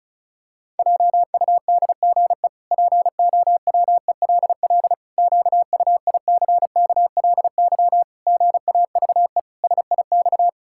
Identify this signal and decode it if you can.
Morse code, heard as a keyed tone, about 36 wpm, 700 Hz. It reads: JUDGE POWELL QUICKLY GAVE SIX
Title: Pangram Challenge v2 - 35wpm